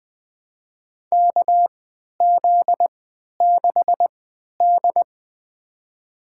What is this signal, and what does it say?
Morse code, heard as a keyed tone, about 20 wpm, spaced as for 11 wpm, 700 Hz. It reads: KZ6D
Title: USA Call Signs - 4 Characters - 3x Character Spacing - 20wpm